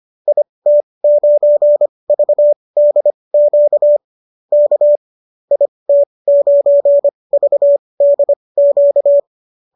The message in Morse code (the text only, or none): IT9VDQ K IT9VDQ